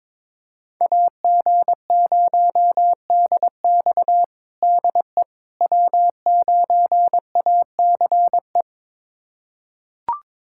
Morse code, heard as a keyed tone, about 22 wpm, 700 Hz. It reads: AG0DX DE W9ACE
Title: USA Callsign DE Callsign - 5 Characters - 22wpm